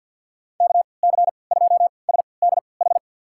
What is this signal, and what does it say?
Morse code, heard as a keyed tone, about 46 wpm, spaced as for 27 wpm, 700 Hz. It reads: KC2SDH
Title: USA Call Signs - 6 Characters - 3x Character Spacing - 50wpm